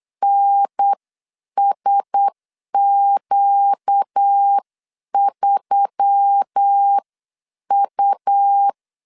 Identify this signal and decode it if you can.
Morse code, heard as a keyed tone, about 8 wpm, spaced as for 7 wpm, 800 Hz. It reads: NSQ3U
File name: CW / DET / 1 6 1500C S C 8wpm 5219